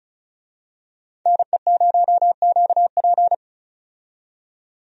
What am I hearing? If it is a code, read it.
Morse code, heard as a keyed tone, about 36 wpm, 700 Hz. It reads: NE0QP
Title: USA Call Signs - 5 Characters - 35wpm